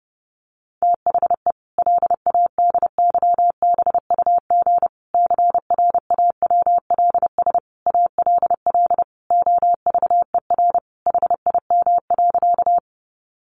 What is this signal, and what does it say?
Morse code, heard as a keyed tone, about 30 wpm, 700 Hz. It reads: THE LADYBUG CRAWLS ALL OVER HIM.
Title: Additional Easy Phrases 30wpm